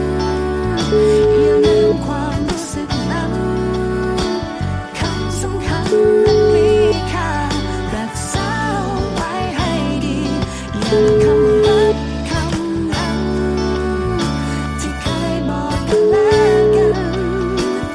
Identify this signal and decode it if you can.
Ringback tone (Europe, 425 Hz), with noise.